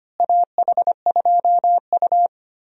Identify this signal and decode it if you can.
Morse code, heard as a keyed tone, about 25 wpm, 700 Hz. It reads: AH2U
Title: USA Call Signs - 4 Characters - Repeated 3x - 25wpm